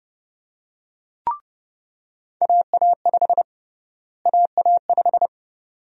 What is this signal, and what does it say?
Morse code, heard as a keyed tone, about 30 wpm, 700 Hz. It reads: AA5 AA5
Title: USA Call Sign Prefixes - Repeated 2x - 30wpm